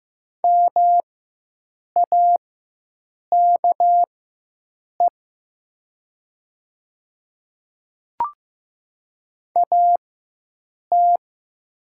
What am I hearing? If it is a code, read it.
Morse code, heard as a keyed tone, about 15 wpm, spaced as for 7 wpm, 700 Hz. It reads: MAKE AT
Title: 4x Character Spacing - Top 100 Words - 15wpm